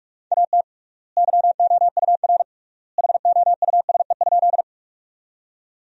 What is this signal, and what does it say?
Morse code, heard as a keyed tone, about 44 wpm, 700 Hz. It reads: AT YOUR HOUSE?
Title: 3x Word-Spacing - Sentences from Top 200 Words - 45wpm